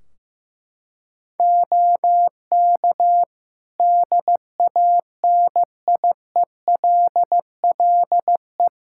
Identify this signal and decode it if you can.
Morse code, heard as a keyed tone, about 16 wpm, 700 Hz. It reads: OK DANIELLE
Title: Practice QSO - Exchange 3 - More Info - 15wpm